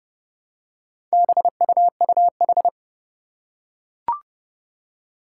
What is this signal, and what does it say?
Morse code, heard as a keyed tone, about 29 wpm, 690 Hz. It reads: BUUH
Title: Four Challenging Letter-Number Combinations 30wpm